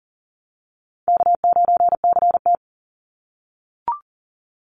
Morse code, heard as a keyed tone, about 38 wpm, 700 Hz. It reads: K9CT
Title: Easy Call Signs - 40wpm